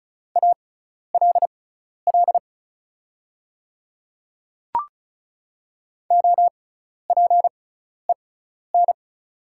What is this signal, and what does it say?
Morse code, heard as a keyed tone, about 34 wpm, spaced as for 12 wpm, 690 Hz. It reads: ALL OPEN
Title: 6x Character Spacing - Top 500 Words - 35wpm